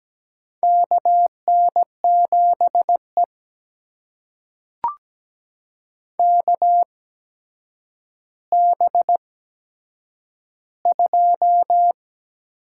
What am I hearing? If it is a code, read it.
Morse code, heard as a keyed tone, about 17 wpm, spaced as for 8 wpm, 700 Hz. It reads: KN7E KB2
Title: USA Call Signs - 8x Character Spacing - 17wpm